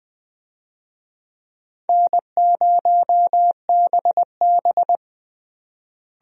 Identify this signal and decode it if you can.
Morse code, heard as a keyed tone, about 20 wpm, 700 Hz. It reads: N0BB
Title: USA Call Signs - 4 Characters - Repeated 3x - 20wpm